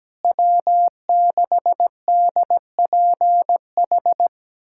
Morse code, heard as a keyed tone, about 17 wpm, 700 Hz. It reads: W6DPH
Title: USA Callsign DE Callsign - 5 Characters - 17wpm